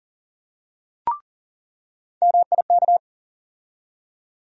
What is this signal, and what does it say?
Morse code, heard as a keyed tone, about 40 wpm, 700 Hz. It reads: MIK
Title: USA Call Sign Postfixes - 3 Letters 40wpm